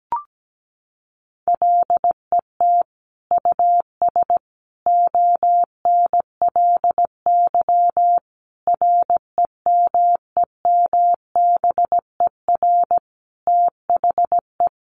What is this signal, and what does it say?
Morse code, heard as a keyed tone, about 17 wpm, 700 Hz. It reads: LET US ONLY REMEMBER THE
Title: Sentences from Top 800 Words 17wpm